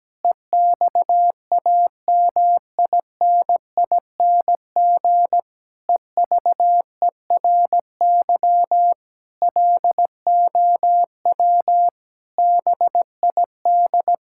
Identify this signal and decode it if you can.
Morse code, heard as a keyed tone, about 17 wpm, 700 Hz. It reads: EXAMINING EVERY LOW BID
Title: Pangram Challenge v2 - 17wpm